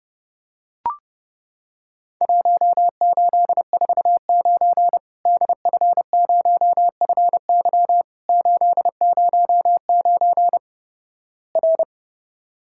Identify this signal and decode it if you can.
Morse code, heard as a keyed tone, about 32 wpm, 700 Hz. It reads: 1849 DF0FY 809 R
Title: Worked All Europe DX Contest - QTC Exchange - 30wpm